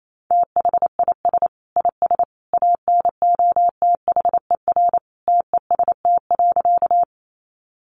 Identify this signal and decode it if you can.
Morse code, heard as a keyed tone, about 28 wpm, 700 Hz. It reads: THIS IS ANOTHER TEST.
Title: Sentences from Top 600 Words 28wpm